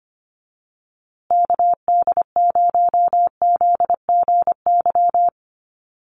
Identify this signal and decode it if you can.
Morse code, heard as a keyed tone, about 25 wpm, 700 Hz. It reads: KD0ZGY
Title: USA Call Signs Repeated 3x 25wpm